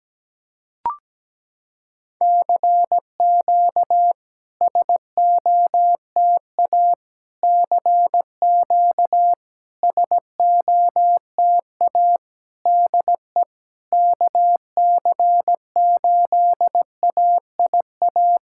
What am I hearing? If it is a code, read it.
Morse code, heard as a keyed tone, about 17 wpm, 700 Hz. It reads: CQ SOTA CQ SOTA DE KC8AIA
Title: SOTA QSO Practice 17wpm